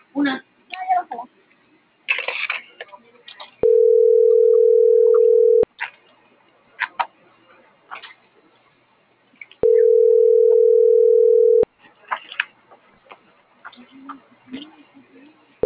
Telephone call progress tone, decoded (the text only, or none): ringback tone